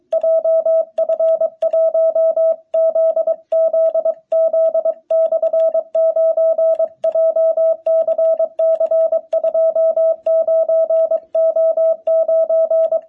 Morse code, heard as a keyed tone, about 23 wpm, 650 Hz. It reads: JF1ZZZ/9 JCC29O9
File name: jh8cbh / 2021cw